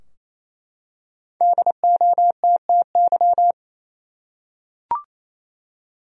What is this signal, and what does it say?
Morse code, heard as a keyed tone, about 28 wpm, 700 Hz. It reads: DOTTY
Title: USA Common Names - Max 5 Letters - 28wpm